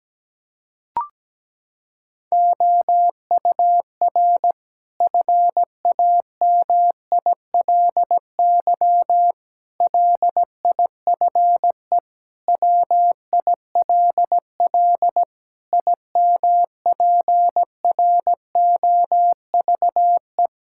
Morse code, heard as a keyed tone, about 17 wpm, 700 Hz. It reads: OUR FAMILY LIFE WILL IMPROVE
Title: Sentences from Top 1000 Words - Review Part 5 - 17wpm